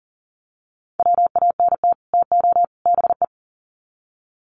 Morse code, heard as a keyed tone, about 40 wpm, 700 Hz. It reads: WANT TO BE
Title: Sets of 3 Words 40wpm